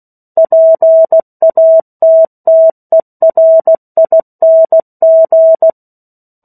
Morse code, heard as a keyed tone, about 16 wpm, 650 Hz. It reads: PATTERING
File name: 03-Chapter-03 16 WPM